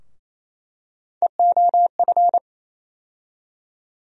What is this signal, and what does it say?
Morse code, heard as a keyed tone, about 28 wpm, 700 Hz. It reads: EOF
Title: USA Call Sign Postfixes - 3 Letters 28wpm